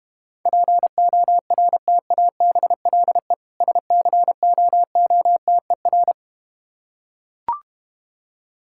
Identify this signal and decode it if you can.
Morse code, heard as a keyed tone, about 32 wpm, 700 Hz. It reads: PORTABLE SCOOTER